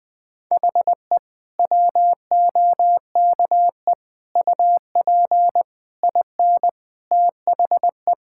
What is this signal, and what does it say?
Morse code, heard as a keyed tone, about 21 wpm, 700 Hz. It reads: HE WOKE UP IN THE
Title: Sentences with 4-Word Phrases - 20wpm